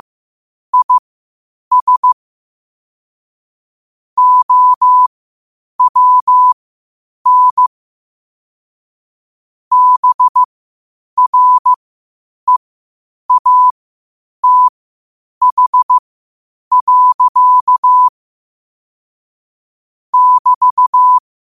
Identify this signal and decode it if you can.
Morse code, heard as a keyed tone, about 15 wpm, spaced as for 8 wpm, 1000 Hz. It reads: IS OWN BREATH. <BT>
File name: qotdfiles / 8wpm